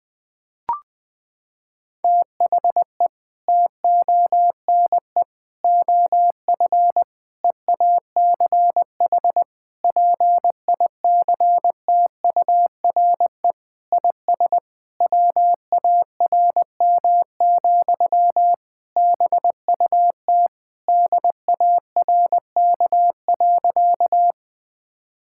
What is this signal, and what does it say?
Morse code, heard as a keyed tone, about 20 wpm, 700 Hz. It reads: THE TONE OF EACH PICTURE IS WARM, BUT DARK.
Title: Sentences from Top 1600 Words 20wpm